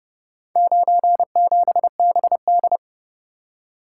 Morse code, heard as a keyed tone, about 30 wpm, 700 Hz. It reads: 97BD